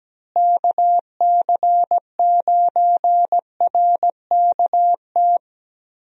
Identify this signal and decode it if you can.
Morse code, heard as a keyed tone, about 17 wpm, 700 Hz. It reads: KC9RKT
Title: USA Call Signs - 6 Characters - Repeated 2x - 17wpm